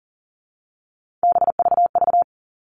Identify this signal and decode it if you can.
Morse code, heard as a keyed tone, about 40 wpm, 700 Hz. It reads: BVV
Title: All Cut Numbers - 3-Digit 40wpm